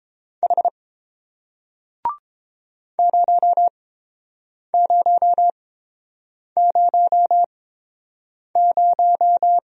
Morse code, heard as a keyed tone, about 30 wpm, spaced as for 7 wpm, 700 Hz. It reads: H 0000